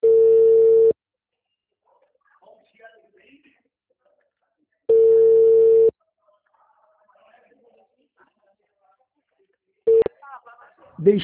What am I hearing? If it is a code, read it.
Ringback tone (China, 450 Hz), with speech.
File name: out- -2016- - - .20